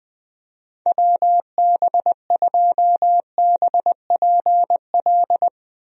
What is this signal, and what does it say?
Morse code, heard as a keyed tone, about 20 wpm, 700 Hz. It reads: WB2BPL
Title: USA Call Signs - 6 Characters - Repeated 3x - 20wpm